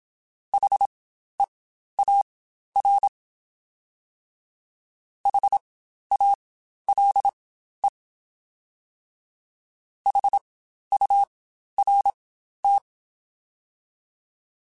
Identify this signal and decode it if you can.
Morse code, heard as a keyed tone, about 27 wpm, spaced as for 12 wpm, 800 Hz. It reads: HEAR HALE HURT
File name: Spaced Alphas